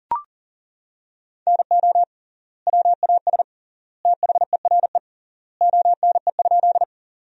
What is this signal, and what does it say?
Morse code, heard as a keyed tone, about 40 wpm, 700 Hz. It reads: NO WAS THERE ONE?
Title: 3x Word-Spacing - Sentences from Top 100 Words - 40wpm